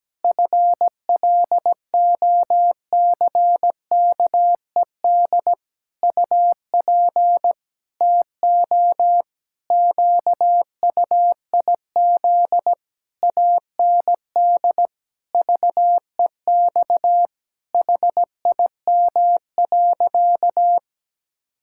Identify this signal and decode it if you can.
Morse code, heard as a keyed tone, about 17 wpm, 700 Hz. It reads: FLOCKED UP TO QUIZ AND VEX HIM.
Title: Pangram Challenge v2 - 17wpm